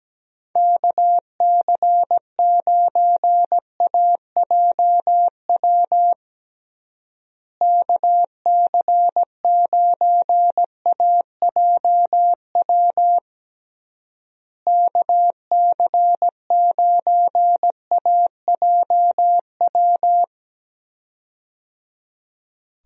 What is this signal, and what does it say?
Morse code, heard as a keyed tone, about 17 wpm, 700 Hz. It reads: KC9AJW KC9AJW KC9AJW
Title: USA Call Signs - 6 Characters - Repeated 3x - 17wpm